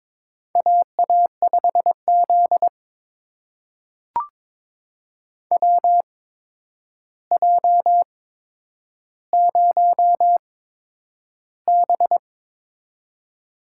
Morse code, heard as a keyed tone, about 22 wpm, spaced as for 9 wpm, 700 Hz. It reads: AA5Z WJ0B
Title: USA Call Signs - 4 Characters - 8x Character Spacing - 22wpm